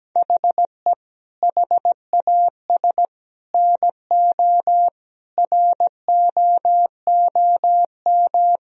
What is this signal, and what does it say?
Morse code, heard as a keyed tone, about 17 wpm, 700 Hz. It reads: HE HAS NO ROOM
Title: Sentences from Top 1000 Words - Review Part 4 - 17wpm